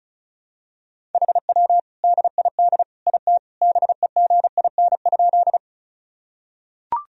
Morse code, heard as a keyed tone, about 36 wpm, 700 Hz. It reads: HW DID IT BEGIN?
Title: Sentences from Top 300 Words and Abbreviations 35wpm